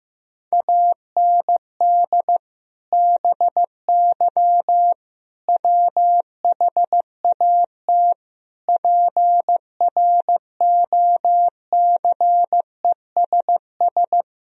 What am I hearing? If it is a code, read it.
Morse code, heard as a keyed tone, about 15 wpm, 700 Hz. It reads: AND BY WHAT PROCESS
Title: Sentences from Top 1000 Words - Review Part 3 - 15wpm